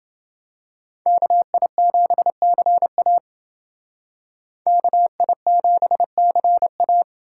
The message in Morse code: KI7CA KI7CA

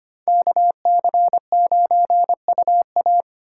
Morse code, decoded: KC9UA

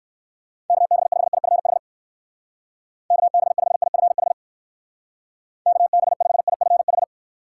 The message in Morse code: KB5IFH KB5IFH KB5IFH

